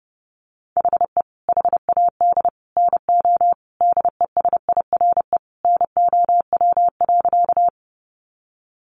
Morse code, heard as a keyed tone, about 30 wpm, 700 Hz. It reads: HE HAD NO DESIRE NOW.
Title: Sentences from Top 1100 Words 30wpm